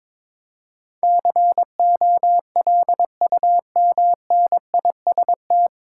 Morse code, heard as a keyed tone, about 22 wpm, 700 Hz. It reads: COLUMNIST
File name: 9 Letter Words - First 3 Letters Given - 22wpm